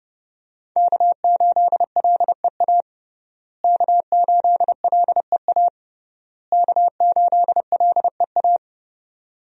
Morse code, heard as a keyed tone, about 30 wpm, 700 Hz. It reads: K8LEA K8LEA K8LEA
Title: USA Call Signs - 5 Characters - Repeated 3x - 30wpm